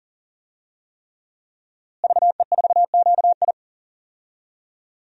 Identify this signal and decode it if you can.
Morse code, heard as a keyed tone, about 40 wpm, 700 Hz. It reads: VE4QI
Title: DXCC-100 Call Signs - 5 Characters - Repeated 3x - 40wpm